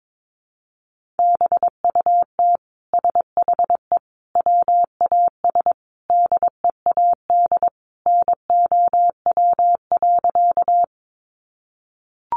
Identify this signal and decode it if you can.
Morse code, heard as a keyed tone, about 22 wpm, 700 Hz. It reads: BUT SHE WAS DEAD NOW.
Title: Sentences from Top 1500 Words 22wpm